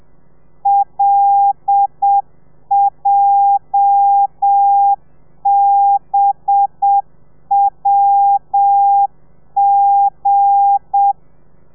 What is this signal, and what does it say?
Morse code, heard as a keyed tone, about 7 wpm, 800 Hz. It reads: LJBWG
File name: telegrafia / i2tfj / Lez8 30